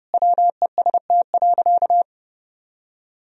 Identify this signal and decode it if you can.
Morse code, heard as a keyed tone, about 30 wpm, 700 Hz. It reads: WEST.
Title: Sentences from Top 1000 Words - Review Part 2 - 30wpm